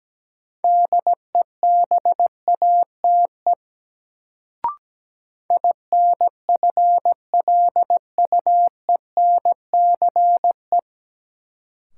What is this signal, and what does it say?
Morse code, heard as a keyed tone, about 17 wpm, 700 Hz. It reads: DEBATE INFLUENCE
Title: Sentences from Top 1100 Words - New Word Drill - 17wpm